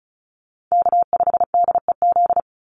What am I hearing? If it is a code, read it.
Morse code, heard as a keyed tone, about 34 wpm, 700 Hz. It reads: K5DEZ
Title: NA SOTA Chaser Callsigns 35wpm